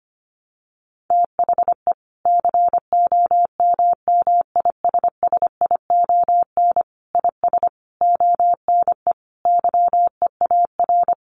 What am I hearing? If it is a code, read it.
Morse code, heard as a keyed tone, about 25 wpm, 700 Hz. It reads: THE COMMISSION IS ONE YEAR
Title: Sentences from Top 1400 Words 25wpm